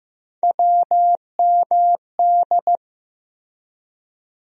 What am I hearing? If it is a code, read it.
Morse code, heard as a keyed tone, about 15 wpm, 700 Hz. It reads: WMD
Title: USA Call Sign Postfixes - 3 Letters 15wpm